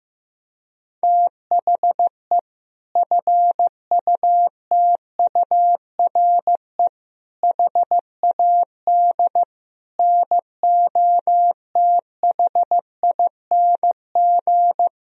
Morse code, heard as a keyed tone, about 15 wpm, 700 Hz. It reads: THE FUTURE HAD NOTHING
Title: Sentences from Top 1000 Words - Review Part 4 - 15wpm